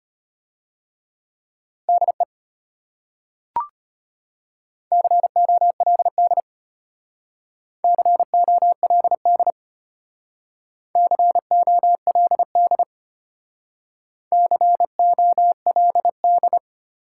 Morse code, heard as a keyed tone, about 32 wpm, 700 Hz. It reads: DE COLD COLD COLD COLD